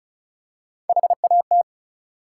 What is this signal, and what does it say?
Morse code, heard as a keyed tone, about 34 wpm, 700 Hz. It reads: HAT
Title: Three Letter Words - 35wpm